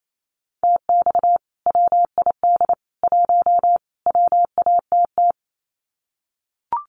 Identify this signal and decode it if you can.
Morse code, heard as a keyed tone, about 28 wpm, 700 Hz. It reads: TX WID 1 WATT
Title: Common QSO Phrases 28wpm